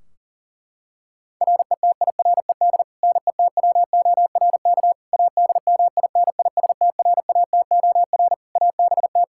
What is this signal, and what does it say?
Morse code, heard as a keyed tone, about 37 wpm, 700 Hz. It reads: RETIRED NETWORK ADMINISTRATOR ABT
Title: Practice QSO - Exchange 3 - More Info - 40wpm